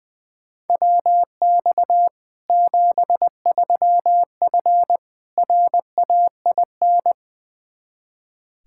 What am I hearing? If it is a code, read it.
Morse code, heard as a keyed tone, about 21 wpm, 700 Hz. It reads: WX 73F RAIN
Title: Practice QSO - Exchange 2 - Info - 20wpm